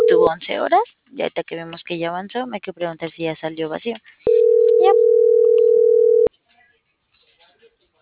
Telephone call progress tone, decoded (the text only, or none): ringback tone